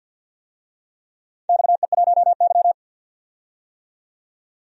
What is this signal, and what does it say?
Morse code, heard as a keyed tone, about 50 wpm, 700 Hz. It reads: XE1Y
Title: DXCC-100 Call Signs - 4 Characters - 50wpm